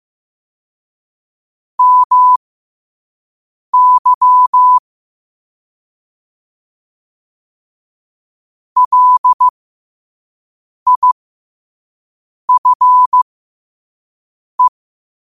Morse code, heard as a keyed tone, about 15 wpm, spaced as for 5 wpm, 1000 Hz. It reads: MY LIFE
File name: 5wpm